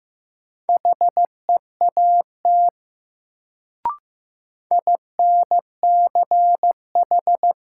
Morse code, heard as a keyed tone, about 15 wpm, 700 Hz. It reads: HEAT INCH
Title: Sentences from Top 1000 Words - Review Part 3 - New Word Drill - 15wpm